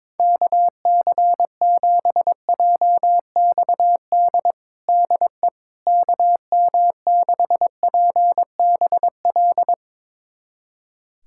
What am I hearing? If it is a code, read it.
Morse code, heard as a keyed tone, about 22 wpm, 700 Hz. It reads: KC7JXD DE KM6PBL
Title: USA Callsign DE Callsign - 6 Characters - 22wpm